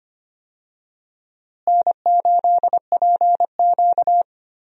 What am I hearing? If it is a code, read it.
Morse code, heard as a keyed tone, about 25 wpm, 700 Hz. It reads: N8PQ
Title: USA Call Signs - 4 Characters - Repeated 2x - 25wpm